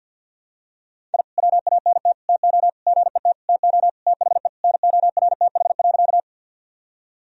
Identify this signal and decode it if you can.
Morse code, heard as a keyed tone, about 47 wpm, 700 Hz. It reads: I WANT TO GET TO THE NORTH.
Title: Sentences from Top 1000 Words - Review Part 1 - 50wpm